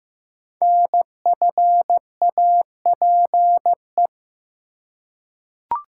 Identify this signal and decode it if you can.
Morse code, heard as a keyed tone, about 15 wpm, 700 Hz. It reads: NFAPE
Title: Code Groups - 5 Letters - 15wpm